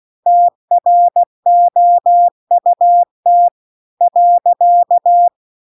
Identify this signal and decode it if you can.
Morse code, heard as a keyed tone, about 16 wpm, 700 Hz. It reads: TROUT .